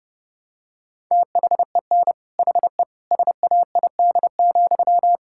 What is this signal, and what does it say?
Morse code, heard as a keyed tone, about 32 wpm, 700 Hz. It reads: THEN HE SAID,
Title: Sentences from Top 1600 Words 30wpm